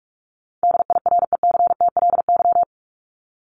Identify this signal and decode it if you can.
Morse code, heard as a keyed tone, about 44 wpm, 700 Hz. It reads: DIRECTLY
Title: Top 501-750 Words 45wpm